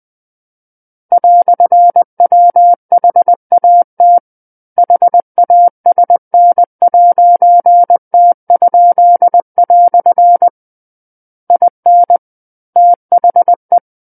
Morse code, heard as a keyed tone, about 20 wpm, 700 Hz. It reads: "WHAT HASN'T?" IN THE
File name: 20 WPM